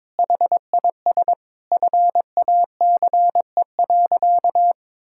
Morse code, heard as a keyed tone, about 22 wpm, 700 Hz. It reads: HIS FACE.
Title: Sentences from Top 1000 Words - Review Part 5 - 22wpm